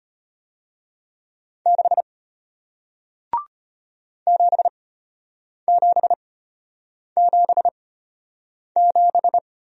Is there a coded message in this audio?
Yes: Morse code from 1.7 to 9.4 s, heard as a keyed tone, about 33 wpm, spaced as for 8 wpm, 700 Hz. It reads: 6 7777